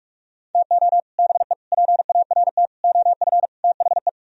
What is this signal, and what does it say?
Morse code, heard as a keyed tone, about 46 wpm, 700 Hz. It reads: TO BE PART OF THE